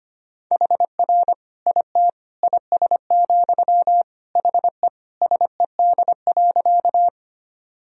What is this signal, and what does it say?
Morse code, heard as a keyed tone, about 25 wpm, 700 Hz. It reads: HR IT IS, HE SED.
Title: Sentences from Top 200 Words and Abbreviations 25wpm